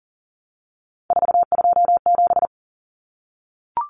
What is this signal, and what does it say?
Morse code, heard as a keyed tone, about 40 wpm, 700 Hz. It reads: 427
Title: Numbers - 3-Digit 40wpm